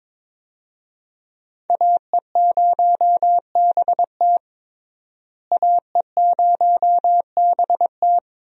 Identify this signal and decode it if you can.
Morse code, heard as a keyed tone, about 22 wpm, 700 Hz. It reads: AE0BT AE0BT
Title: USA Call Signs - 5 Characters - Repeated 2x - 22wpm